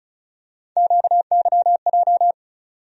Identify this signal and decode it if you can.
Morse code, heard as a keyed tone, about 36 wpm, 700 Hz. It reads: QYJ
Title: USA Call Sign Postfixes - 3 Letters 35wpm